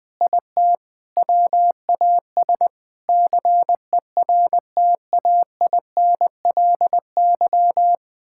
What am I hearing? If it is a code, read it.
Morse code, heard as a keyed tone, about 21 wpm, 700 Hz. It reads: IT WAS CERTAINLY
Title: Sentences from Top 1000 Words - Review Part 6 - 20wpm